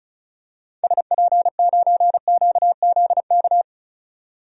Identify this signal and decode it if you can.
Morse code, heard as a keyed tone, about 36 wpm, 700 Hz. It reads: SP9QZK
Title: DXCC-100 Call Signs - 6 Characters - 35wpm